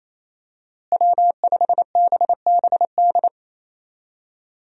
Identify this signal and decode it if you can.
Morse code, heard as a keyed tone, about 28 wpm, 700 Hz. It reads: W5BBD
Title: USA Call Signs - 5 Characters - Repeated 2x - 28wpm